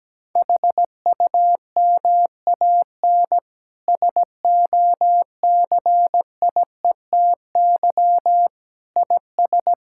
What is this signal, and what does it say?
Morse code, heard as a keyed tone, about 17 wpm, 700 Hz. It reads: HUMAN SOCIETY IS
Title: Sentences from Top 1000 Words - Review Part 6 - 17wpm